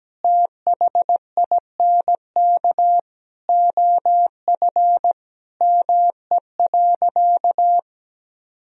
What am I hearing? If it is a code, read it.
Morse code, heard as a keyed tone, about 17 wpm, 700 Hz. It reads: THINK OF ME.
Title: Sentences from Top 1000 Words - Review Part 1 - 17wpm